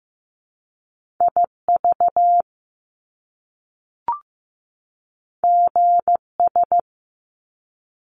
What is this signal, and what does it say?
Morse code, heard as a keyed tone, about 15 wpm, 700 Hz. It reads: IV GS
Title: Top Two Letter Combinations in Words 15wpm